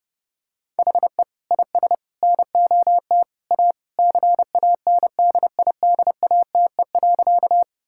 Morse code, heard as a keyed tone, about 30 wpm, 700 Hz. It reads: HE IS NOT A CANDIDATE.
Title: Sentences from Top 1600 Words 30wpm